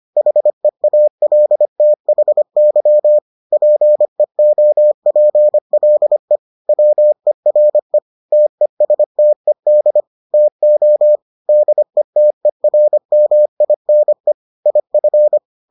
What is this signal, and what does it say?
Morse code, heard as a keyed tone, about 25 wpm, 600 Hz. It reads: HEALTHY PEOPLE WERE TESTED TO DETERMINE IF